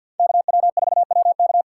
Morse code, heard as a keyed tone, about 53 wpm, 700 Hz. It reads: KW4WK
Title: USA Call Signs - 5 Characters - 8x Character Spacing - 50wpm